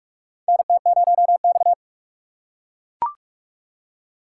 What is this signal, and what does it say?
Morse code, heard as a keyed tone, about 44 wpm, 700 Hz. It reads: NT0X
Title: USA Call Signs - 4 Characters - 3x Character Spacing - 45wpm